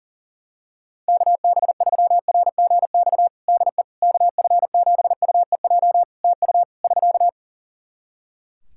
Morse code, heard as a keyed tone, about 38 wpm, 700 Hz. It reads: KB3RGX DE KF7UEJ TU <SK>
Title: Practice QSO - Exchange 4 - Ending - 40wpm